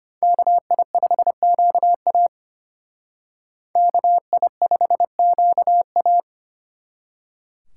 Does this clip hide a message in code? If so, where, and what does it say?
Yes: Morse code from 0.2 to 6.2 s, heard as a keyed tone, about 32 wpm, 700 Hz. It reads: KI5QA KI5QA